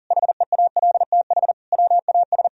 Morse code, heard as a keyed tone, about 37 wpm, 700 Hz. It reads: HEALTH WAS
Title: Sentences from Top 1000 Words - Review Part 3 - 40wpm